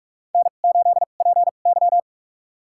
Morse code, heard as a keyed tone, about 46 wpm, spaced as for 32 wpm, 700 Hz. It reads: N8PY
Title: USA Call Signs - 2x Character Spacing - 45wpm